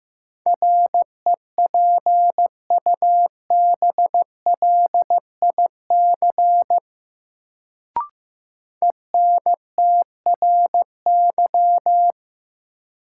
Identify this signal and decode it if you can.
Morse code, heard as a keyed tone, about 15 wpm, 700 Hz. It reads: REPUBLIC ENTRY
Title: Sentences from Top 1500 Words - New Word Drill Alt - 15wpm